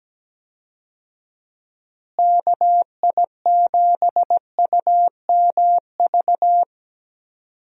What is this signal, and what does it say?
Morse code, heard as a keyed tone, about 17 wpm, 700 Hz. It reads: KI7UMV
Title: USA Call Signs - 6 Characters - Repeated 3x - 17wpm